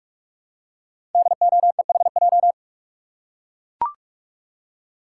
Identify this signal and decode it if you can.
Morse code, heard as a keyed tone, about 44 wpm, 690 Hz. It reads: DOEHJ
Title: Code Groups - 5 Letters - 45wpm